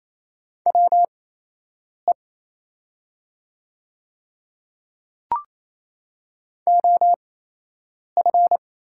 Morse code, heard as a keyed tone, about 28 wpm, spaced as for 8 wpm, 700 Hz. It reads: WE OF
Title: 8x Character Spacing - Top 100 Words - 28wpm